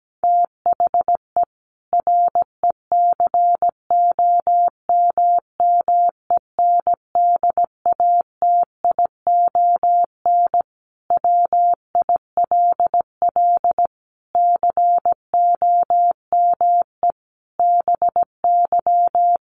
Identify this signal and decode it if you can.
Morse code, heard as a keyed tone, about 17 wpm, 700 Hz. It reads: THE RECOMMENDATION WILL COME BY
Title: Sentences from Top 1400 Words 17wpm